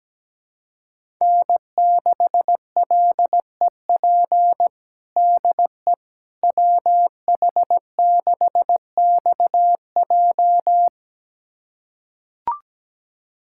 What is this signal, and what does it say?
Morse code, heard as a keyed tone, about 17 wpm, 700 Hz. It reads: N6LEP DE WH6XJ
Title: USA Callsign DE Callsign - 5 Characters - 17wpm